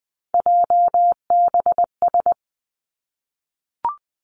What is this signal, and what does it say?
Morse code, heard as a keyed tone, about 20 wpm, 700 Hz. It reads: JBS
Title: Three Challenging Letter-Number Combinations 20wpm